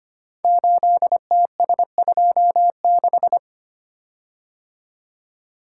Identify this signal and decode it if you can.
Morse code, heard as a keyed tone, about 25 wpm, 700 Hz. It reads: 8TS26
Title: Code Groups - 5 Letter:Numbers - 25wpm